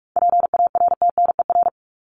Morse code, heard as a keyed tone, about 44 wpm, 700 Hz. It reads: PARTNER
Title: Top 1501-1750 Words 45wpm